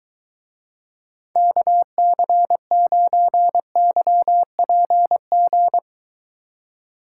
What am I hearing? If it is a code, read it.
Morse code, heard as a keyed tone, about 23 wpm, 700 Hz. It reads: KC9YPG